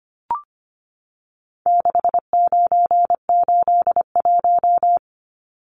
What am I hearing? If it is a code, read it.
Morse code, heard as a keyed tone, about 25 wpm, 700 Hz. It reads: 6981
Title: Numbers - 4-Digit 25wpm